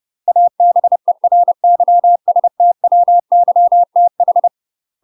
Morse code, heard as a keyed tone, about 30 wpm, 700 Hz. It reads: ABERYSTWYTH
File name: UK MASTER QSOs TRAINING/30WPM